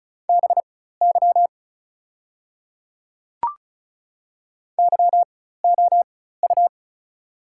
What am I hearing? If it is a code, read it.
Morse code, heard as a keyed tone, about 36 wpm, spaced as for 16 wpm, 700 Hz. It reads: BY YOU
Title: 4x Character Spacing - Top 200 Words - 35wpm